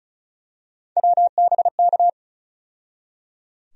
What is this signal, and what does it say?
Morse code, heard as a keyed tone, about 36 wpm, 700 Hz. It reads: WBK
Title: USA Call Sign Postfixes - 3 Letters 35wpm